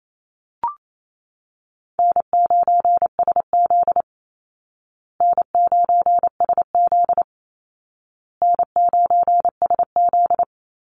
Morse code, heard as a keyed tone, about 28 wpm, 700 Hz. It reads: N9SZ N9SZ N9SZ
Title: Easy Call Signs Repeated 3x - 28wpm